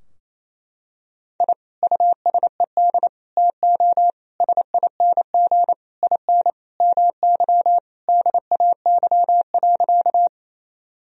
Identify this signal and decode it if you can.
Morse code, heard as a keyed tone, about 28 wpm, 700 Hz. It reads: I USED TO SING IN MY DAY.
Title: Sentences from Top 1000 Words - Review Part 2 - 28wpm